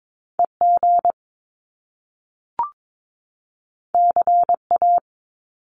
Morse code, heard as a keyed tone, about 22 wpm, 700 Hz. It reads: EG CA
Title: Top Two Letter Combinations in Words 22wpm